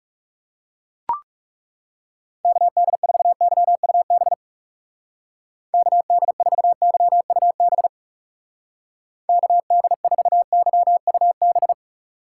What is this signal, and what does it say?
Morse code, heard as a keyed tone, about 36 wpm, 700 Hz. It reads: KD4YUB KD4YUB KD4YUB
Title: Speed Racing - USA 2x3 Call Signs 30wpm